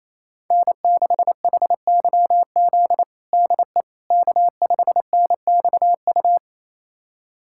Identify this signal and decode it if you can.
Morse code, heard as a keyed tone, about 28 wpm, 700 Hz. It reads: N6HYZ DE K5NXU
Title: USA Callsign DE Callsign - 5 Characters - 28wpm